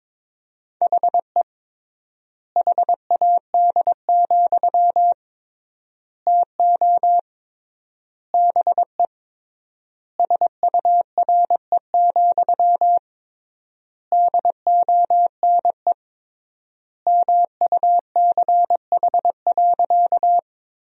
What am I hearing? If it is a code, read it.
Morse code, heard as a keyed tone, about 22 wpm, 700 Hz. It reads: HE HAD, TO BE SURE, DONE MUCH.
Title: 3x Word-Spacing - Sentences from Top 500 Words - 22wpm